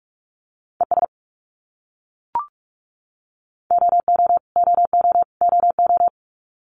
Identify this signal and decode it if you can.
Morse code, heard as a keyed tone, about 44 wpm, 700 Hz. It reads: ES OO OO OO
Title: Abbreviations Repeated 3x - 45wpm